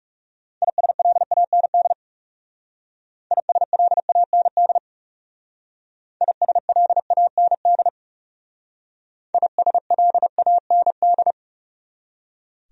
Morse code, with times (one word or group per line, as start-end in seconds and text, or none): 0.6-1.9 ISLAND
3.3-4.8 ISLAND
6.2-7.9 ISLAND
9.3-11.3 ISLAND